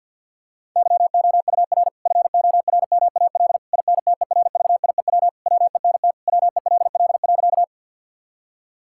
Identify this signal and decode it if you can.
Morse code, heard as a keyed tone, about 50 wpm, 700 Hz. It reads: YOUR FORMAL INTERVIEW WENT WELL.
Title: Sentences from Top 1600 Words 50wpm